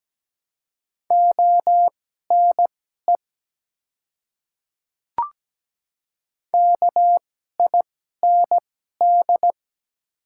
Morse code, heard as a keyed tone, about 17 wpm, spaced as for 12 wpm, 700 Hz. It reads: ONE KIND
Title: 2x Character Spacing - Top 200 Words - 17wpm